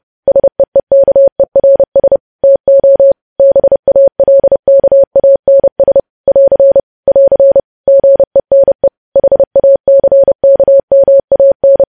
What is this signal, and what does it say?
Morse code, heard as a keyed tone, about 30 wpm, 550 Hz. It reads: SEEKERS TO BALKANS <AR> <AR> GENE HACKMAN
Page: news headlines in morse code at 30 wpm